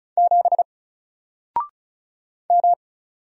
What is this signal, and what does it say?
Morse code, heard as a keyed tone, about 35 wpm, 700 Hz. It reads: 7 M
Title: Single Character (Letter:Number:Punctuation:Prosign) 35wpm